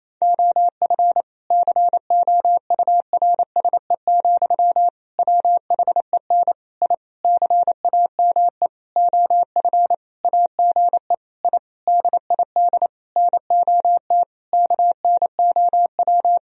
Morse code, heard as a keyed tone, about 28 wpm, 700 Hz. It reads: OF COURSE, WHEN I CAME OF AGE I DID NOT KNOW
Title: Sentences from Top 1000 Words - Review Part 3 - 28wpm